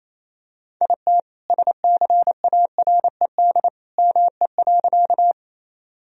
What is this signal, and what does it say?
Morse code, heard as a keyed tone, about 28 wpm, 700 Hz. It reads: IT SCARED ME.
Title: Sentences from Top 1000 Words - Review Part 4 - 28wpm